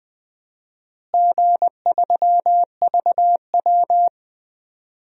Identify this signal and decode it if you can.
Morse code, heard as a keyed tone, about 20 wpm, 700 Hz. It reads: G3VW
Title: DXCC-100 Call Signs - 4 Characters - Repeated 2x - 20wpm